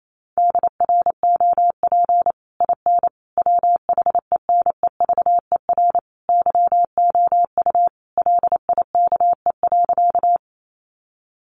Morse code, heard as a keyed tone, about 28 wpm, 700 Hz. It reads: DROP IN WHENEVER YOU LIKE.
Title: Sentences from Top 1400 Words 28wpm